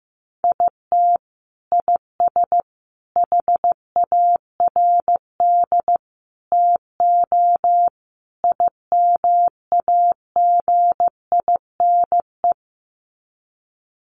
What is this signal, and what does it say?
Morse code, heard as a keyed tone, about 15 wpm, 700 Hz. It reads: IT IS HARD TO IMAGINE
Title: Sets of 5 Words Encore2 15wpm